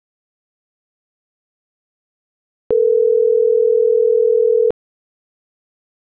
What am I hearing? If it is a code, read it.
Ringback tone (North America), 440 + 480 Hz.